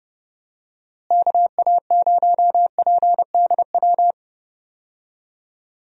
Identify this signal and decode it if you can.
Morse code, heard as a keyed tone, about 30 wpm, 700 Hz. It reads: KA0PDW
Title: USA Call Signs - 6 Characters - Repeated 3x - 30wpm